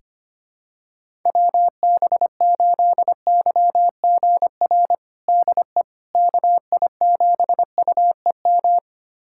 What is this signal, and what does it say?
Morse code, heard as a keyed tone, about 25 wpm, 700 Hz. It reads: WB8YGR DE KI7UEM
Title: USA Callsign DE Callsign - 6 Characters - 25wpm